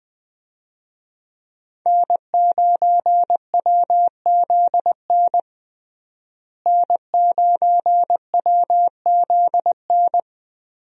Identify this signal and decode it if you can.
Morse code, heard as a keyed tone, about 19 wpm, 700 Hz. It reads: N9WZN N9WZN
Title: USA Call Signs - 5 Characters - Repeated 3x - 20wpm